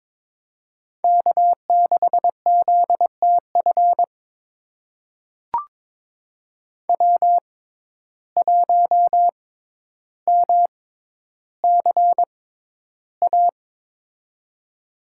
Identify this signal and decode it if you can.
Morse code, heard as a keyed tone, about 22 wpm, spaced as for 11 wpm, 700 Hz. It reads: K6ZTF W1MCA
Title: USA Call Signs - 5 Characters - 6x Character Spacing - 22wpm